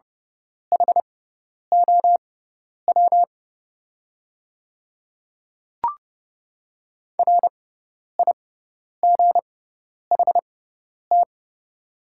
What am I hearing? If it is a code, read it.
Morse code, heard as a keyed tone, about 31 wpm, spaced as for 10 wpm, 700 Hz. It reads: HOW RIGHT
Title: 6x Character Spacing - Top 200 Words - 30wpm